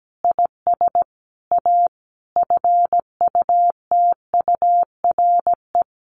Morse code, heard as a keyed tone, about 17 wpm, 700 Hz. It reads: IS A FUTURE
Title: Sentences from Top 700 Words 17wpm